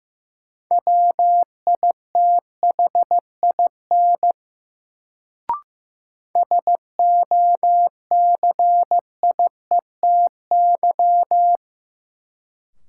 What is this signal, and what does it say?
Morse code, heard as a keyed tone, about 16 wpm, 700 Hz. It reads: WITHIN SOCIETY
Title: Sentences from Top 700 Words - New Word Drill - 15wpm